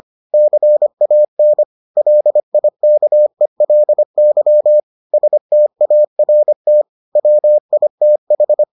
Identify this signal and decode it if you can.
Morse code, heard as a keyed tone, about 25 wpm, 600 Hz. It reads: CAN LIKELY START WITH